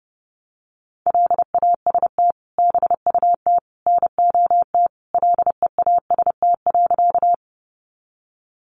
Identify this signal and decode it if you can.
Morse code, heard as a keyed tone, about 30 wpm, 700 Hz. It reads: LAST BUT NOT LEAST.
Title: Sentences from Top 700 Words 30wpm